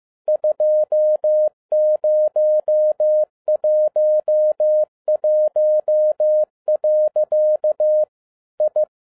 Morse code, heard as a keyed tone, about 15 wpm, 610 Hz. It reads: 2011. I
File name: 2012-01-mp3 / n13-A NEWLY DESIGNED QSL CARD-15wpm